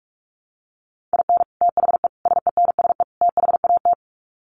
Morse code, heard as a keyed tone, about 44 wpm, 700 Hz. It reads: IN THE SENSE THAT
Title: Sets of 4 Words 45wpm